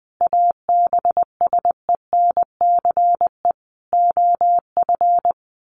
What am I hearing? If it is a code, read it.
Morse code, heard as a keyed tone, about 20 wpm, 700 Hz. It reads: ABSENCE OF
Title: Sets of 4 Words 20wpm